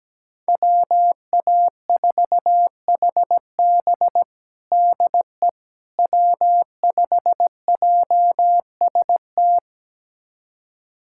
Morse code, heard as a keyed tone, about 17 wpm, 700 Hz. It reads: WA4HB DE W5JST
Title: USA Callsign DE Callsign - 5 Characters - 17wpm